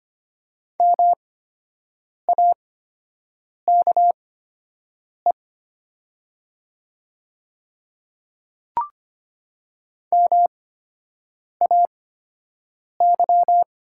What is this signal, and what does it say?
Morse code, heard as a keyed tone, about 25 wpm, spaced as for 7 wpm, 700 Hz. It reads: MAKE MAY
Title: 8x Character Spacing - Top 100 Words - 25wpm